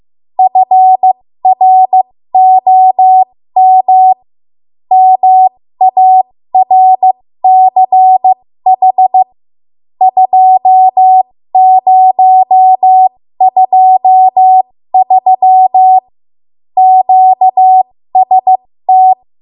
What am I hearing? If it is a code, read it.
Morse code, heard as a keyed tone, about 15 wpm, spaced as for 13 wpm, 750 Hz. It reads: FROM MARCH 2023 QST
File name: Morse / 13WPM